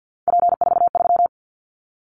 Morse code, heard as a keyed tone, about 50 wpm, 700 Hz. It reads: P43
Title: Callsign Prefixes for Top 100 DXCC - 50wpm